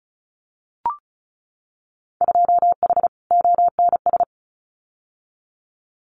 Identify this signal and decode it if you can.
Morse code, heard as a keyed tone, about 36 wpm, 700 Hz. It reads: 2H ONS
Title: Winter Field Day Exchanges Encore 35wpm